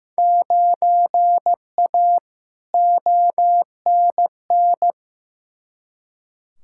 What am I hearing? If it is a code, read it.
Morse code, heard as a keyed tone, about 15 wpm, 700 Hz. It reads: 9A ONN
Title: Field Day Exchanges Encore - 15wpm